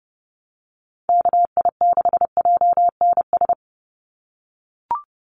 Morse code, heard as a keyed tone, about 30 wpm, 700 Hz. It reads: KI6JNS
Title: USA Call Signs 30wpm